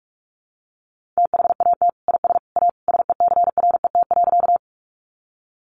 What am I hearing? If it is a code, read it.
Morse code, heard as a keyed tone, about 44 wpm, 700 Hz. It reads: THAT IS A SECRET.
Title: Sentences from Top 1200 Words 45wpm